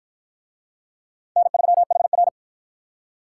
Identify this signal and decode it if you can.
Morse code, heard as a keyed tone, about 51 wpm, 700 Hz. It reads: N3HR